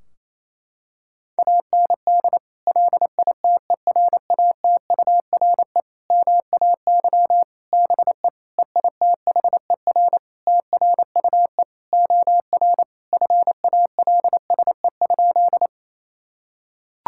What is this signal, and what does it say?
Morse code, heard as a keyed tone, about 28 wpm, 700 Hz. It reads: AND LITERATURE MAY BE EITHER TRUE OR FALSE?
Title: Sentences from Top 1600 Words 28wpm